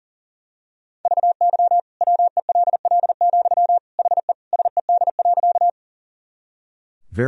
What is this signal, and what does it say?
Morse code, heard as a keyed tone, about 37 wpm, 700 Hz. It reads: VY WELL, HE SED.
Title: Sentences from Top 200 Words and Abbreviations 40wpm